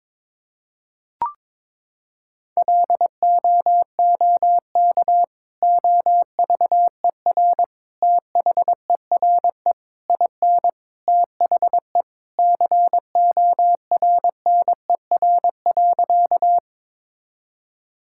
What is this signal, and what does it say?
Morse code, heard as a keyed tone, about 22 wpm, 700 Hz. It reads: LOOK OVER THERE IN THE CORNER.
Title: Sentences from Top 1600 Words 22wpm